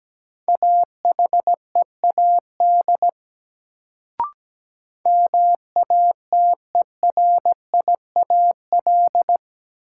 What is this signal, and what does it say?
Morse code, heard as a keyed tone, about 17 wpm, 700 Hz. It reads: AHEAD MATERIAL
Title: Sentences from Top 1100 Words - New Word Drill Alt - 17wpm